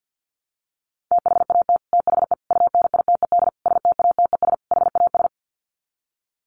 Morse code, heard as a keyed tone, about 48 wpm, 700 Hz. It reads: THAT THE UNITED STATES HAS
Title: Sets of 5 Words Encore 50wpm